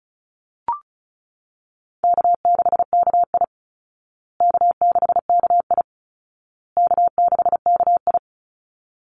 Morse code, heard as a keyed tone, about 34 wpm, 700 Hz. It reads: K6KI K6KI K6KI
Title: Easy Call Signs Repeated 3x - 35wpm